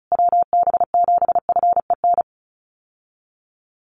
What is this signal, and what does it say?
Morse code, heard as a keyed tone, about 34 wpm, 700 Hz. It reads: WB7FEN
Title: USA Call Signs Repeated 3x 35wpm